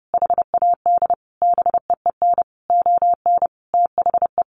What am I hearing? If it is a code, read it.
Morse code, heard as a keyed tone, about 30 wpm, 700 Hz. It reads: HAD BEEN ON THE
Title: Sentences from Top 1400 Words 30wpm